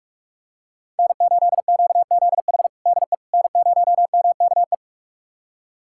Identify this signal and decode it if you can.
Morse code, heard as a keyed tone, about 41 wpm, 700 Hz. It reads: N8QZH DE N0MKE
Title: USA Callsign DE Callsign - 5 Characters - 45wpm